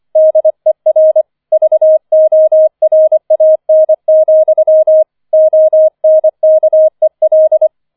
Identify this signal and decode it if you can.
Morse code, heard as a keyed tone, about 25 wpm, 630 Hz. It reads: DER VORAN, ONKEL
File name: 005 Fünfter Streich 25wpm